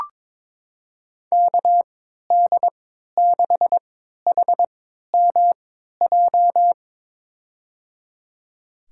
Morse code, heard as a keyed tone, about 22 wpm, spaced as for 12 wpm, 700 Hz. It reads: KD6HMJ
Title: USA Call Signs - 6 Characters - 3x Character Spacing - 22wpm